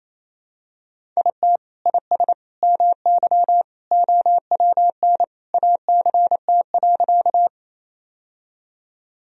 Morse code, heard as a keyed tone, about 28 wpm, 700 Hz. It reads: IT IS MY OWN ACT.
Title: Sentences from Top 1000 Words - Review Part 1 - 28wpm